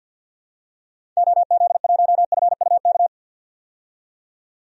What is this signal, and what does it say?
Morse code, heard as a keyed tone, about 52 wpm, 700 Hz. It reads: OZ1FUK
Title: DXCC-100 Call Signs - 6 Characters - 50wpm